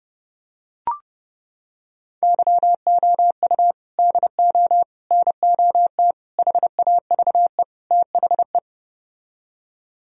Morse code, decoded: YOU DO NOT HAVE THE

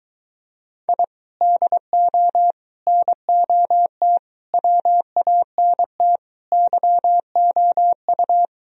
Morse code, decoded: I DO NOT WANT YOU

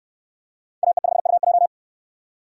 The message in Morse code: AE5UJ

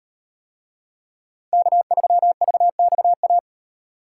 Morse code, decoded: K3VXA

